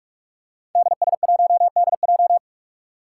Morse code, heard as a keyed tone, about 44 wpm, 700 Hz. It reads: DS1DJ
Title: Five Challenging Letter-Number Combinations 45wpm